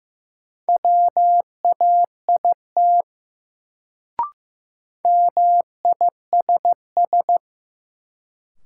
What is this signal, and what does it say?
Morse code, heard as a keyed tone, about 15 wpm, 700 Hz. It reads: WAIT MISS
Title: Sentences from Top 500 Words - New Word Drill - 15wpm